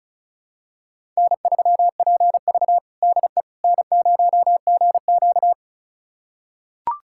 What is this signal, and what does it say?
Morse code, heard as a keyed tone, about 34 wpm, 700 Hz. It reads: N3PV DE N0GQ
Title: USA Callsign DE Callsign - 4 Characters - 35wpm